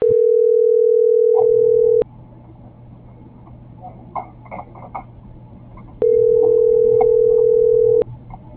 Ringback tone (North America), 440 + 480 Hz.